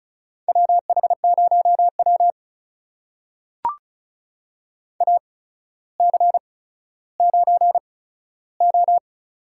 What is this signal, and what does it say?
Morse code, heard as a keyed tone, about 36 wpm, spaced as for 15 wpm, 700 Hz. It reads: WH0W AC9O
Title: USA Call Signs - 4 Characters - 8x Character Spacing - 35wpm